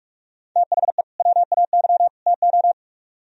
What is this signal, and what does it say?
Morse code, heard as a keyed tone, about 44 wpm, 700 Hz. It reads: THE WAY TO